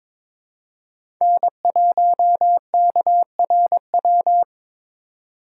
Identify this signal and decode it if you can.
Morse code, heard as a keyed tone, about 22 wpm, 700 Hz. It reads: N1KRW
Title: USA Call Signs - 5 Characters - 4x Character Spacing - 22wpm